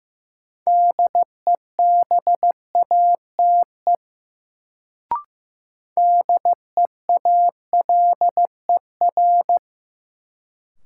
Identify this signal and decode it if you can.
Morse code, heard as a keyed tone, about 15 wpm, 700 Hz. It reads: DEBATE DEALER
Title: Sentences from Top 1100 Words - New Word Drill Alt - 15wpm